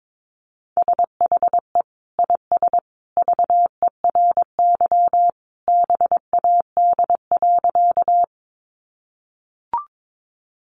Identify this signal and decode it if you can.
Morse code, heard as a keyed tone, about 22 wpm, 700 Hz. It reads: SHE IS VERY BAD.
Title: Sentences from Top 700 Words 22wpm